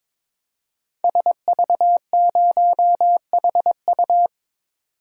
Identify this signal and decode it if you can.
Morse code, heard as a keyed tone, about 21 wpm, 700 Hz. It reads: SV0HU
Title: DXCC-100 Call Signs - 5 Characters - 22wpm